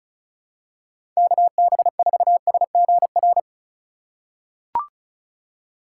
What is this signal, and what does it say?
Morse code, heard as a keyed tone, about 36 wpm, 700 Hz. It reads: KB4SGR
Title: USA Call Signs - 3x Character Spacing - 35wpm